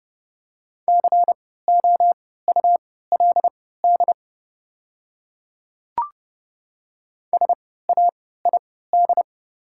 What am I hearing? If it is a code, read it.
Morse code, heard as a keyed tone, about 30 wpm, spaced as for 17 wpm, 700 Hz. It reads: COULD SAID
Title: 3x Character Spacing - Top 100 Words - 30wpm